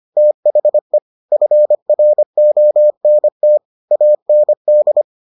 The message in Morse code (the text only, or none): THE FRONT AND